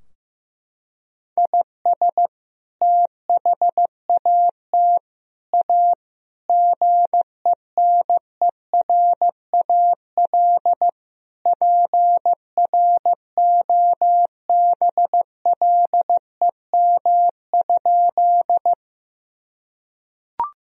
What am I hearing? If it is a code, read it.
Morse code, heard as a keyed tone, about 16 wpm, 700 Hz. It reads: IS THAT A GENERAL PROBLEM?
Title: Sentences from Top 1000 Words - Review Part 3 - 15wpm